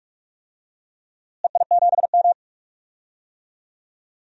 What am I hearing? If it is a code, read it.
Morse code, heard as a keyed tone, about 46 wpm, 700 Hz. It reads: EI7M
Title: DXCC-100 Call Signs - 4 Characters - Repeated 3x - 45wpm